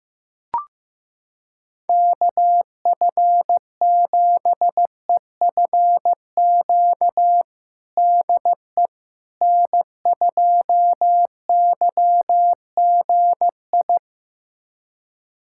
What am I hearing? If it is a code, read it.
Morse code, heard as a keyed tone, about 15 wpm, 700 Hz. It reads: KF7EFQ DE N2YGI
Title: USA Callsign DE Callsign 15wpm